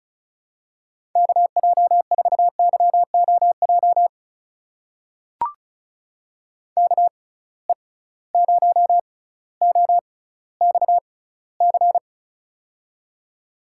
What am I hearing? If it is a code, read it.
Morse code, heard as a keyed tone, about 36 wpm, spaced as for 18 wpm, 700 Hz. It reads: KJ4YOJ KE0OXC
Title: USA Call Signs - 6 Characters - 6x Character Spacing - 35wpm